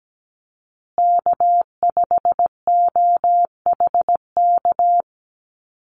Morse code, heard as a keyed tone, about 17 wpm, 700 Hz. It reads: K5OHK
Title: USA Call Signs Repeated 2x 17wpm